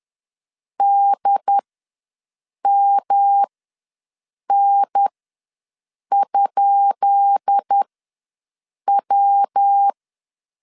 Morse code, heard as a keyed tone, about 10 wpm, spaced as for 6 wpm, 800 Hz. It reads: DMN?W
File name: LEZIONI / 10 SIGNS / 10B INT ws 10wpm 4134